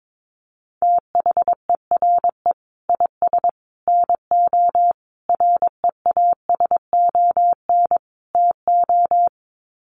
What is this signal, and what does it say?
Morse code, heard as a keyed tone, about 22 wpm, 700 Hz. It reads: THERE IS NO REASON TO
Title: Sets of 5 Words Encore 22wpm